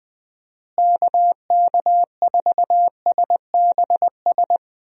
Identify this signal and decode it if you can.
Morse code, heard as a keyed tone, about 20 wpm, 700 Hz. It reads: KK4SBS
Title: USA Call Signs - 6 Characters - Repeated 3x - 20wpm